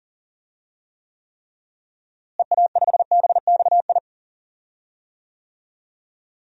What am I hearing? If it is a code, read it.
Morse code, heard as a keyed tone, about 38 wpm, 700 Hz. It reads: EA5BXI
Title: DXCC-100 Call Signs - 6 Characters - Repeated 3x - 40wpm